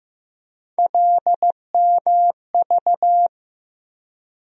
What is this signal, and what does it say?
Morse code, heard as a keyed tone, about 15 wpm, 700 Hz. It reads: LMV
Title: USA Call Sign Postfixes - 3 Letters 15wpm